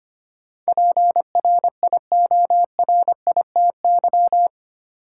Morse code, heard as a keyed tone, about 25 wpm, 700 Hz. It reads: PRIORITY
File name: 8 Letter Words - First 3 Letters Given - 25wpm